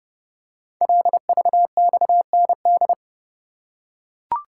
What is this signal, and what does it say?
Morse code, heard as a keyed tone, about 30 wpm, 700 Hz. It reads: LVXND
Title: Code Groups - 5 Letters - 30wpm